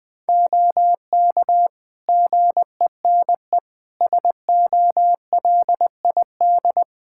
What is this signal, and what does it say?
Morse code, heard as a keyed tone, about 21 wpm, 700 Hz. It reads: OK GENE SOLID
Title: Practice QSO - Exchange 3 - More Info - 20wpm